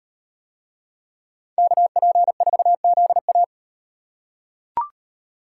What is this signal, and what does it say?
Morse code, heard as a keyed tone, about 38 wpm, 700 Hz. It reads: KP4ZA